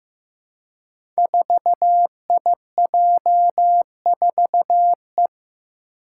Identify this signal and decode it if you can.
Morse code, heard as a keyed tone, about 15 wpm, 700 Hz. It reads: 4IJ4E
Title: Five Challenging Letter-Number Combinations 15wpm